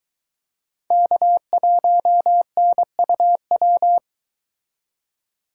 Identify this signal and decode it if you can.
Morse code, heard as a keyed tone, about 21 wpm, 700 Hz. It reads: K1NUW
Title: Speed Racing - USA 1x3 Call Signs 17wpm